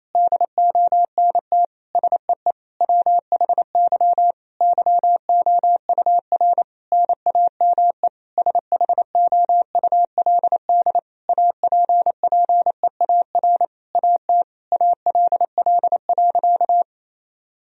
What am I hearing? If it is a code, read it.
Morse code, heard as a keyed tone, about 27 wpm, 700 Hz. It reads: DONT SEE WHY YOUR NAME SHOULD APPEAR AT ALL.
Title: Sentences from Top 1000 Words - Review Part 2 - 28wpm